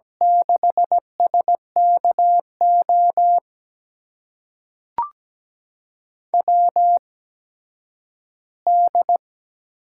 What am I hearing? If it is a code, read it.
Morse code, heard as a keyed tone, about 17 wpm, spaced as for 10 wpm, 700 Hz. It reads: 6SKO WD
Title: USA Call Signs - 6 Characters - 8x Character Spacing - 17wpm